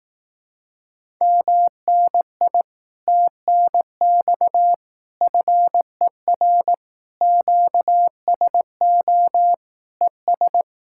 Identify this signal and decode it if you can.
Morse code, heard as a keyed tone, about 18 wpm, 700 Hz. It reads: MNI TNX FER QSO ES
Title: Speed-Racing Common QSO Phrases Encore - 15wpm